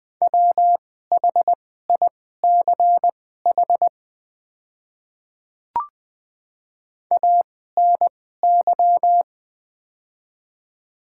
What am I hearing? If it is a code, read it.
Morse code, heard as a keyed tone, about 21 wpm, spaced as for 15 wpm, 700 Hz. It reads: WHICH ANY
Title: 2x Character Spacing - Top 300 Words - 20wpm